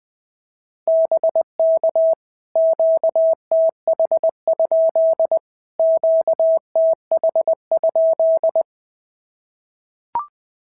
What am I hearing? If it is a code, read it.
Morse code, heard as a keyed tone, about 19 wpm, 650 Hz. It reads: BK QTH? QTH?
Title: POTA QSO Practice - 20wpm